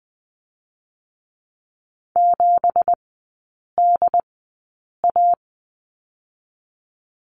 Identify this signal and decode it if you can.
Morse code, heard as a keyed tone, about 20 wpm, spaced as for 8 wpm, 690 Hz. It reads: 7DA
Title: 172 Lesson 28e - Number 9 - Three Characters 20wpm